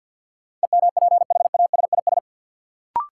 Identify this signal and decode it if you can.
Morse code, heard as a keyed tone, about 50 wpm, 700 Hz. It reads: EMPHASIS
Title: Sentences from Top 1400 Words - New Word Drill - 50wpm